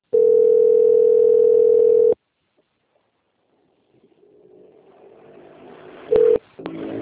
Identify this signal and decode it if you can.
Ringback tone (North America), 440 + 480 Hz.